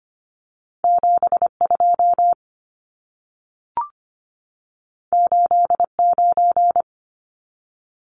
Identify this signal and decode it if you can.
Morse code, heard as a keyed tone, about 25 wpm, 700 Hz. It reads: 72 89
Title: Numbers - 2-Digit 25wpm